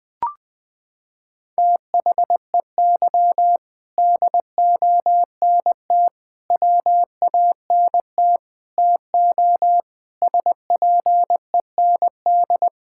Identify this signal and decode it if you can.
Morse code, heard as a keyed tone, about 21 wpm, 700 Hz. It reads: THEY DONT WANT TO SPEND
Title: Sentences from Top 1000 Words - Review Part 5 - 20wpm